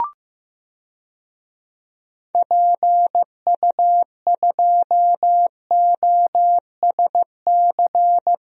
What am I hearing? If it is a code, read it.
Morse code, heard as a keyed tone, about 15 wpm, 700 Hz. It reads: PU2OSC
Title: DXCC-100 Call Signs - 6 Characters - Repeated 2x - 15wpm